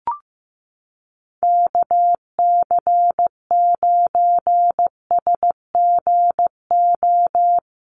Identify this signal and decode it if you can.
Morse code, heard as a keyed tone, about 15 wpm, 700 Hz. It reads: KC9SGO
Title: USA Call Signs Repeated 3x 15wpm